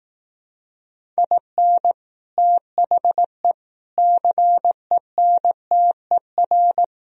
Morse code, heard as a keyed tone, about 18 wpm, 700 Hz. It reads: IN THE CENTER
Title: Speed-Racing - Sets of 4 Words 15wpm